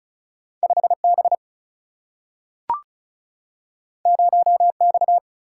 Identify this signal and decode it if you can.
Morse code, heard as a keyed tone, about 36 wpm, 700 Hz. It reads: 5 B 0 X
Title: Single-Letter Single-Number 35wpm